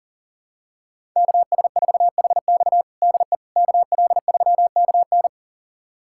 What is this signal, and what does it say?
Morse code, heard as a keyed tone, about 40 wpm, 700 Hz. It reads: KS4HX DE KL3KN
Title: USA Callsign DE Callsign - 5 Characters - 40wpm